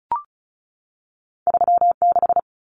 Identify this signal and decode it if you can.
Morse code, heard as a keyed tone, about 36 wpm, 700 Hz. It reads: 36
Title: Common Cut Numbers - 2-Digit 35wpm